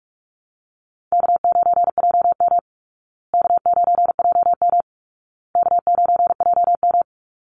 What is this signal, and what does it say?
Morse code, heard as a keyed tone, about 44 wpm, 700 Hz. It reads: K9JM K9JM K9JM
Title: Call Signs Repeated 2x - 45wpm